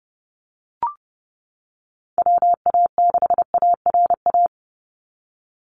Morse code, heard as a keyed tone, about 30 wpm, 700 Hz. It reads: WA6ARA
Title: NA SOTA Chaser Callsigns 30wpm